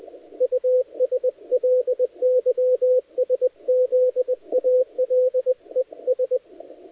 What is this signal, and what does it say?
Morse code, heard as a keyed tone, about 20 wpm, 510 Hz, with radio noise. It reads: USLYSZALES
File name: dlazaawansowanych / nagraniazpasma / titawaNETcontrol